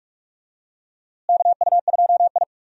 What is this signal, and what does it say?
Morse code, heard as a keyed tone, about 41 wpm, 700 Hz. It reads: KU2I
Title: USA Call Signs - 4 Characters - Repeated 2x - 45wpm